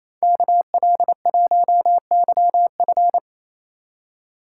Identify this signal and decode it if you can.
Morse code, heard as a keyed tone, about 33 wpm, 700 Hz. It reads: KL1YF
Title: Speed Racing - USA Call Signs 28wpm